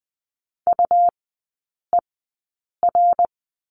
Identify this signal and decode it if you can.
Morse code, heard as a keyed tone, about 19 wpm, spaced as for 8 wpm, 700 Hz. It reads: UER
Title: 100 Lesson 17e - Letter u - Three Characters 20wpm